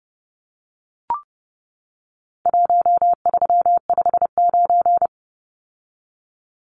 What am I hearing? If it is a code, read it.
Morse code, heard as a keyed tone, about 29 wpm, 700 Hz. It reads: 1359
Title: Numbers - Contest Style - 4-Digit 30wpm